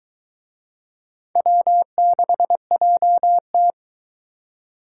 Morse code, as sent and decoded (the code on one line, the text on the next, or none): .-- -.... .--- -
W6JT